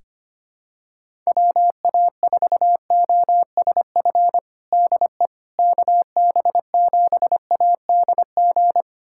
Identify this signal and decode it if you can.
Morse code, heard as a keyed tone, about 25 wpm, 700 Hz. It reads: WA4OSF DE KB7ADG
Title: USA Callsign DE Callsign - 6 Characters - 25wpm